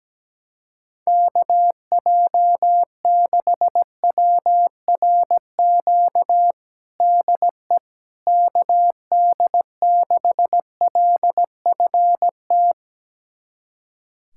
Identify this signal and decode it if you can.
Morse code, heard as a keyed tone, about 17 wpm, 700 Hz. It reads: KJ6WRQ DE KD6LFT
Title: USA Callsign DE Callsign - 6 Characters - 17wpm